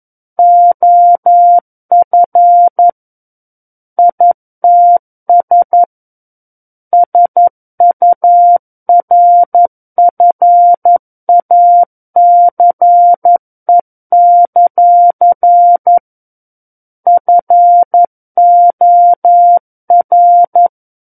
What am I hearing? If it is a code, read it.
Morse code, heard as a keyed tone, about 11 wpm, 700 Hz. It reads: OF ITS SURFACE; FOR
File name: Book One / 02-Chapter 2 - 11 WPM